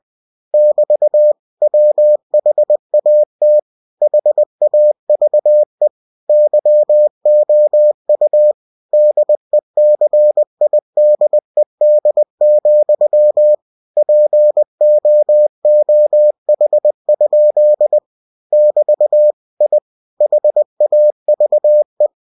Morse code, heard as a keyed tone, about 20 wpm, 600 Hz. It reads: <BT> WHAT HAVE YOU DECIDED, POOH? <BT> I HAVE
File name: cw-books / winnie-the-pooh / winnie-20wpm-page-0034